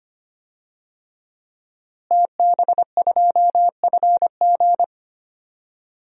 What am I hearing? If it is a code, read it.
Morse code, heard as a keyed tone, about 25 wpm, 700 Hz. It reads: TB2FG
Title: DXCC-100 Call Signs - 5 Characters - Repeated 3x - 25wpm